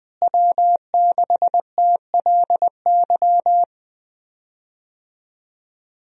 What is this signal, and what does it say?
Morse code, heard as a keyed tone, about 20 wpm, 700 Hz. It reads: W6TLY
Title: USA Call Signs - 5 Characters - Repeated 3x - 20wpm